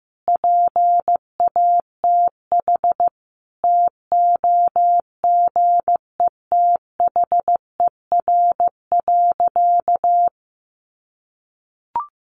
Morse code, heard as a keyed tone, about 15 wpm, 700 Hz. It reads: PATH TOGETHER.
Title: Sentences from Top 1400 Words 15wpm